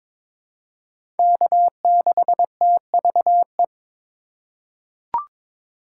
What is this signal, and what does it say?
Morse code, heard as a keyed tone, about 22 wpm, 700 Hz. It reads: K6TVE
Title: USA Call Signs - 5 Characters - 3x Character Spacing - 22wpm